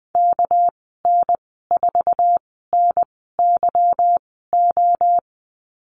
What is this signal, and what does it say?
Morse code, heard as a keyed tone, about 19 wpm, spaced as for 14 wpm, 700 Hz. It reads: KN4NYO
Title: 200 Lesson 32g - Letter k - Call Signs 20wpm